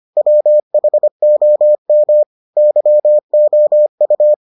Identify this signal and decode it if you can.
Morse code, heard as a keyed tone, about 25 wpm, 600 Hz. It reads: WHOM YOU